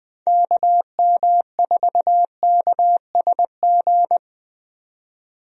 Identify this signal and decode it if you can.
Morse code, heard as a keyed tone, about 20 wpm, 700 Hz. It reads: KM4KSG
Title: USA Call Signs - 6 Characters - 20wpm